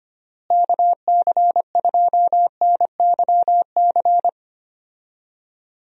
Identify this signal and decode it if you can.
Morse code, heard as a keyed tone, about 25 wpm, 700 Hz. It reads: KC2NYC
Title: USA Call Signs - 6 Characters - Repeated 3x - 25wpm